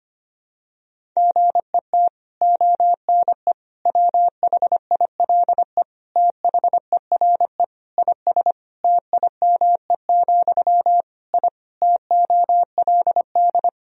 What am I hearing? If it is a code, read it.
Morse code, heard as a keyed tone, about 25 wpm, 700 Hz. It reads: GET ONE WHILE THERE IS TIME, I TOLD
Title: Sentences from Top 1000 Words - Review Part 1 - 25wpm